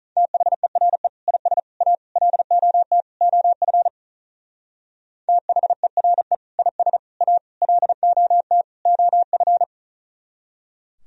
Morse code, heard as a keyed tone, about 42 wpm, 700 Hz. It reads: THERE IS A LOT OF THERE IS A LOT OF
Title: Speed-Racing - Sets of 5 Words 35wpm